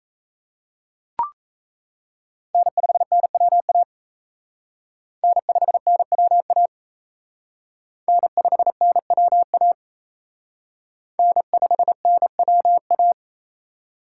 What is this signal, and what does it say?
Morse code, heard as a keyed tone, about 35 wpm, 700 Hz. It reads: N5NWA N5NWA N5NWA N5NWA
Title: Speed Racing - USA Call Signs 28wpm